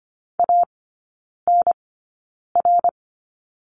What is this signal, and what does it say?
Morse code, heard as a keyed tone, about 25 wpm, spaced as for 9 wpm, 700 Hz. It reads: ANR
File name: course / 25:10wpm / 093 Lesson 16e - Letter c - Three Characters 25wpm